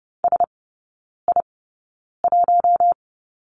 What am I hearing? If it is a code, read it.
Morse code, heard as a keyed tone, about 30 wpm, spaced as for 9 wpm, 700 Hz. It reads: SI1
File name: 081 Lesson 14e - Number 2 - Three Characters 30wpm